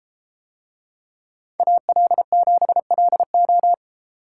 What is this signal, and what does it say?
Morse code, heard as a keyed tone, about 33 wpm, 700 Hz. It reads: AL7LO